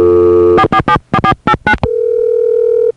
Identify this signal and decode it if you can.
DTMF, heard as telephone touch tones, tones of about 70 ms. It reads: 4474489